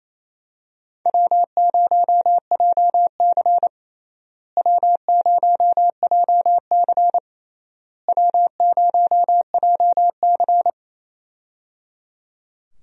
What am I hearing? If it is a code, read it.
Morse code, heard as a keyed tone, about 28 wpm, 700 Hz. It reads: W0JC W0JC W0JC
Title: USA Call Signs - 4 Characters - Repeated 3x - 28wpm